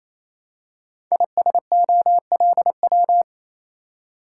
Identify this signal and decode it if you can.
Morse code, heard as a keyed tone, about 27 wpm, 690 Hz. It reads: ISOLW
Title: Code Groups - 5 Letter:Numbers - 28wpm